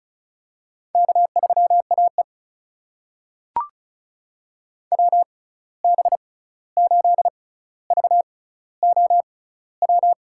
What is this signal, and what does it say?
Morse code, heard as a keyed tone, about 36 wpm, spaced as for 16 wpm, 700 Hz. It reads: K3AE WB8VOW
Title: USA Call Signs - 6x Character Spacing - 35wpm